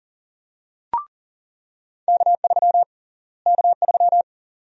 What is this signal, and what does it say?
Morse code, heard as a keyed tone, about 38 wpm, 700 Hz. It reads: K3 K3
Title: USA Call Sign Prefixes - Repeated 2x - 40wpm